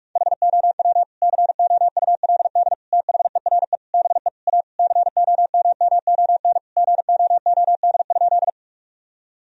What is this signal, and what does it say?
Morse code, heard as a keyed tone, about 43 wpm, 700 Hz. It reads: HOW COULD THERE BE A COMMON GOOD?
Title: Sentences from Top 1000 Words - Review Part 3 - 45wpm